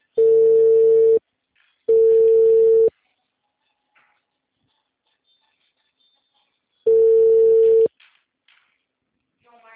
Ringback tone (China, 450 Hz).
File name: out- -2024- - - .18